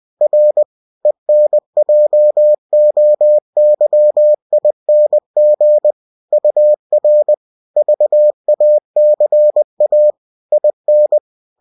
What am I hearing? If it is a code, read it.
Morse code, heard as a keyed tone, about 20 wpm, 600 Hz. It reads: R ENJOYING UR VACA IN